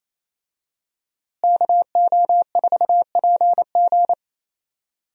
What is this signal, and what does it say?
Morse code, heard as a keyed tone, about 28 wpm, 700 Hz. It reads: KO4PG
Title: USA Call Signs - 5 Characters - Repeated 3x - 28wpm